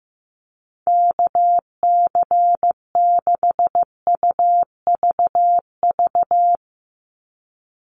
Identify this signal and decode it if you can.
Morse code, heard as a keyed tone, about 15 wpm, 700 Hz. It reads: KC6UVV
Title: USA Call Signs 15wpm